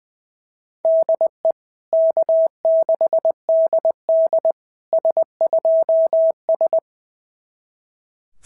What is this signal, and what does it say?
Morse code, heard as a keyed tone, about 20 wpm, 690 Hz. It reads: DE K6DD S2S
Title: SOTA QSO Practice 20wpm